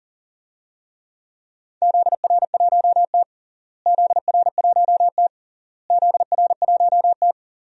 Morse code, heard as a keyed tone, about 40 wpm, 700 Hz. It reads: ZR1T ZR1T ZR1T
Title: DXCC-100 Call Signs - 4 Characters - Repeated 3x - 40wpm